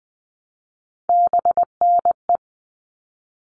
Morse code, heard as a keyed tone, about 20 wpm, 700 Hz. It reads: BNE
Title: All Cut Numbers - 3-Digit 20wpm